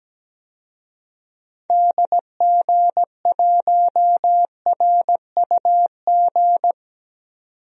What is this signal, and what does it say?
Morse code, heard as a keyed tone, about 17 wpm, 700 Hz. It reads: DG1RUG
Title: DXCC-100 Call Signs - 6 Characters - 17wpm